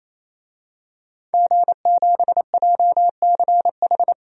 Morse code, heard as a keyed tone, about 28 wpm, 700 Hz. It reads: G7JCH
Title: DXCC-100 Call Signs - 5 Characters - 28wpm